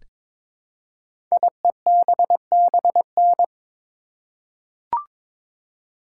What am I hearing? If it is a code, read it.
Morse code, heard as a keyed tone, about 23 wpm, 700 Hz. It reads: IEBBN
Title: Code Groups - 5 Letters - 22wpm